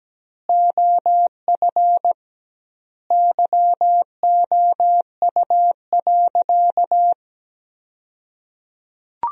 Morse code, heard as a keyed tone, about 17 wpm, 700 Hz. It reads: OF YOU.
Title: 2x Word-Spacing - Sentences from Top 500 Words - 17wpm